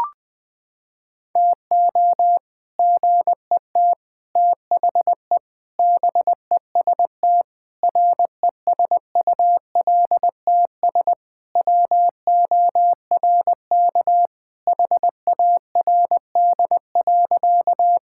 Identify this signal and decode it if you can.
Morse code, heard as a keyed tone, about 20 wpm, 700 Hz. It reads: TO GET THE BEST RESULTS WORK HARD.
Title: Sentences with 3-Word Phrases - 20wpm